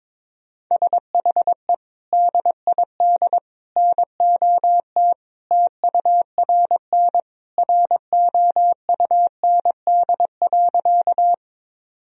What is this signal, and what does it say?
Morse code, heard as a keyed tone, about 22 wpm, 700 Hz. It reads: SHE DID NOT TURN ROUND.
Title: Sentences from Top 1000 Words - Review Part 1 - 22wpm